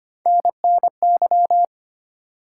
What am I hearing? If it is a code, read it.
Morse code, heard as a keyed tone, about 25 wpm, 700 Hz. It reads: NNY
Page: morse-code-ninja-practice-25wpm